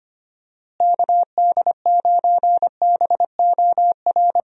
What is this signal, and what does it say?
Morse code, heard as a keyed tone, about 25 wpm, 700 Hz. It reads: KD9BOR
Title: USA Call Signs - 6 Characters - Repeated 3x - 25wpm